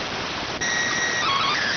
SSTV (slow-scan television), PD 290.